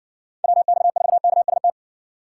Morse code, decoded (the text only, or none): WX4KST